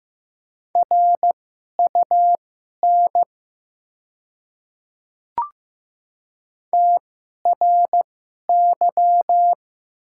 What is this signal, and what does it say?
Morse code, heard as a keyed tone, about 15 wpm, spaced as for 11 wpm, 700 Hz. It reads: RUN TRY
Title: 2x Character Spacing - Top 300 Words - 15wpm